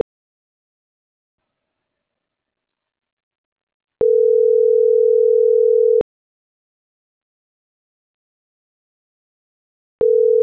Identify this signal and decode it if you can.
Ringback tone (North America), 440 + 480 Hz.